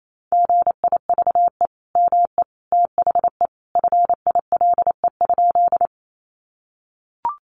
Morse code, heard as a keyed tone, about 28 wpm, 700 Hz. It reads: GIVE ME THE FILE?
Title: Sentences from Top 1200 Words 28wpm